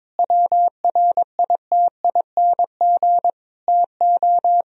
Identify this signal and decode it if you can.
Morse code, heard as a keyed tone, about 22 wpm, 700 Hz. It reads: WRITING TO
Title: Sentences from Top 1000 Words - Review Part 4 - 22wpm